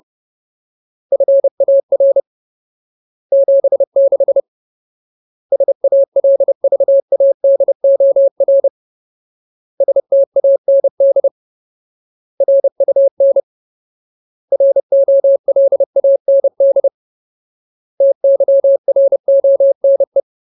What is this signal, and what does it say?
Morse code, heard as a keyed tone, about 30 wpm, 550 Hz. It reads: FAR 76 SALVADOR STAND RUN ROLAND TYRONE
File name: Ultimate Word List / 30WPM Ultimate